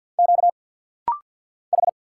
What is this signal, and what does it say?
Morse code, heard as a keyed tone, about 50 wpm, 700 Hz. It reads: Q H
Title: Single Letter-Number Alt - Rapid-Fire 50wpm